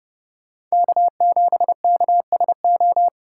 Morse code, heard as a keyed tone, about 39 wpm, 700 Hz. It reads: K7KSO